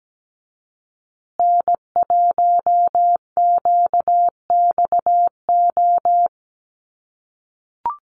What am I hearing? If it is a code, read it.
Morse code, heard as a keyed tone, about 17 wpm, 700 Hz. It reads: N1QXO
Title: USA Call Signs Repeated 2x 17wpm